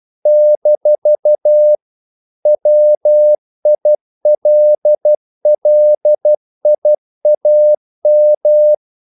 Morse code, heard as a keyed tone, about 12 wpm, 600 Hz. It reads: - WILLIAM
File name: CW / Practice / EnglishQuotations0312